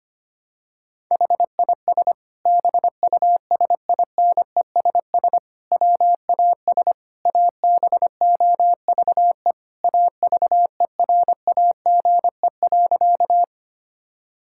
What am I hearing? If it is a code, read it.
Morse code, heard as a keyed tone, about 25 wpm, 700 Hz. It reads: HIS BUSINESS WAS ABOVE AVERAGE.
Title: Sentences from Top 1000 Words - Review Part 6 - 25wpm